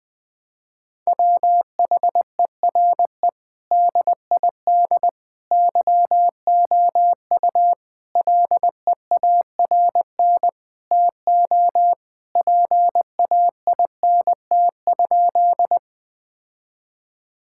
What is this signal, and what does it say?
Morse code, heard as a keyed tone, about 20 wpm, 700 Hz. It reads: WHERE DID YOU LEARN TO PAINT?
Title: Sentences from Top 1000 Words - Review Part 6 - 20wpm